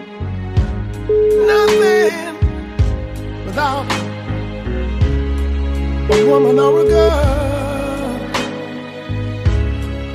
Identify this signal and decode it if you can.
Ringback tone (Europe, 425 Hz), with noise.